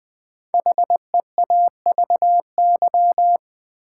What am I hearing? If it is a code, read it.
Morse code, heard as a keyed tone, about 20 wpm, 700 Hz. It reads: HEAVY
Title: Sentences from Top 1000 Words - Review Part 6 - New Word Drill - 20wpm